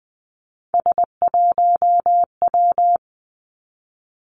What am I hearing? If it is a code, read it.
Morse code, heard as a keyed tone, about 20 wpm, 700 Hz. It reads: S1W
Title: Three Challenging Letter-Number Combinations 20wpm